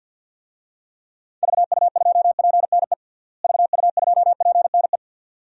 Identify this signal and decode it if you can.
Morse code, heard as a keyed tone, about 52 wpm, 700 Hz. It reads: VU2PNE VU2PNE
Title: DXCC-100 Call Signs - 6 Characters - Repeated 2x - 50wpm